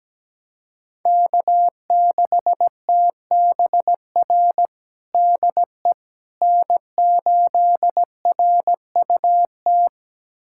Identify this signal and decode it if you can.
Morse code, heard as a keyed tone, about 17 wpm, 700 Hz. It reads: K6TBR DE N8RUT
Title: USA Callsign DE Callsign - 5 Characters - 17wpm